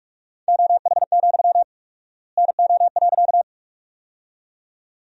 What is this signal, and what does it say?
Morse code, heard as a keyed tone, about 46 wpm, 700 Hz. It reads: OH, NO.
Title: 4x Word-Spacing - Sentences from Top 500 Words - 45wpm